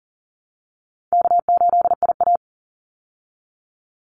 Morse code, heard as a keyed tone, about 40 wpm, 700 Hz. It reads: K8IA
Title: Easy Call Signs Repeated 2x - 40wpm